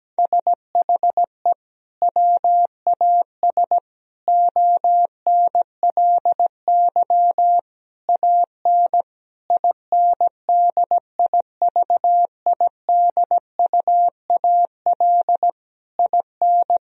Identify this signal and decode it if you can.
Morse code, heard as a keyed tone, about 17 wpm, 700 Hz. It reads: SHE WAS ONLY AN INDIVIDUAL IN
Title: Sentences from Top 1000 Words - Review Part 4 - 17wpm